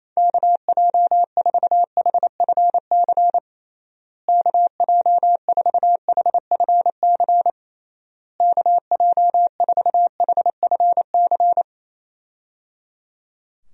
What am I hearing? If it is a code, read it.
Morse code, heard as a keyed tone, about 28 wpm, 700 Hz. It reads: KJ4HFC KJ4HFC KJ4HFC
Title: USA Call Signs - 6 Characters - Repeated 3x - 28wpm